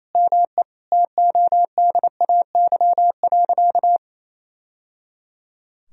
Morse code, heard as a keyed tone, about 28 wpm, 700 Hz. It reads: ME TODAY.
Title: Sentences from Top 1000 Words - Review Part 4 - 28wpm